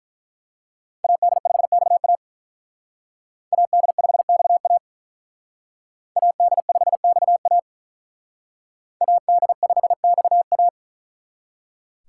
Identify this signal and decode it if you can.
Morse code, heard as a keyed tone, about 42 wpm, 700 Hz. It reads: AD5XA AD5XA AD5XA AD5XA
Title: Speed Racing - USA 2x2 Call Signs 35wpm